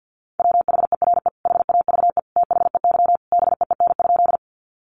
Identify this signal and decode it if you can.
Morse code, heard as a keyed tone, about 48 wpm, 700 Hz. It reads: WHERE HAVE THEY BEEN?
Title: Sentences from Top 200 Words 50wpm